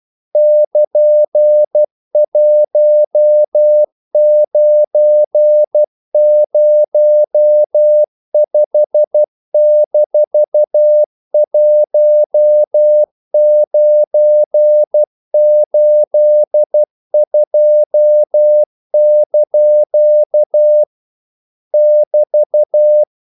Morse code, heard as a keyed tone, about 12 wpm, 600 Hz. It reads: <KN>1905-1982) <BT>
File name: CW / Practice / EnglishQuotations0112